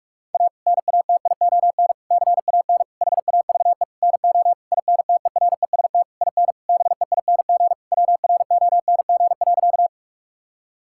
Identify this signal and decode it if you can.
Morse code, heard as a keyed tone, about 46 wpm, 700 Hz. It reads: A NATION CAN HAVE NO INTEREST IN BEING WRONG.
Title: Sentences from Top 1000 Words - Review Part 6 - 45wpm